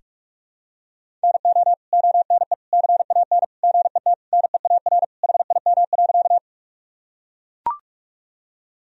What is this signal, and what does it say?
Morse code, heard as a keyed tone, about 41 wpm, 700 Hz. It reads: NO ONE CAN GET NEAR HIM.
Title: Sentences from Top 1000 Words - Review Part 1 - 45wpm